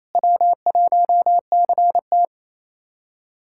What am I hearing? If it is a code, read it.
Morse code, heard as a keyed tone, about 28 wpm, 700 Hz. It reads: W1CT
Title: USA Callsign DE Callsign - 4 Characters - 28wpm